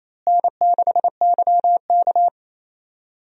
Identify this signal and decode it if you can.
Morse code, heard as a keyed tone, about 27 wpm, 700 Hz. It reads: N6YK
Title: USA Call Signs - 4 Characters - Repeated 3x - 28wpm